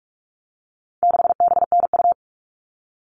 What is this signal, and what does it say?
Morse code, heard as a keyed tone, about 44 wpm, 700 Hz. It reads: 6BNU
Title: All Cut Numbers - 4-Digit 45wpm